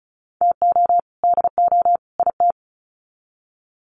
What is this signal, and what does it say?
Morse code, heard as a keyed tone, about 34 wpm, 700 Hz. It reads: TO DO IT
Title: Sets of 3 Words 35wpm